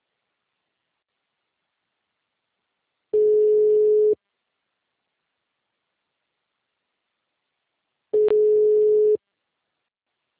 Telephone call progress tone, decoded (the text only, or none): ringback tone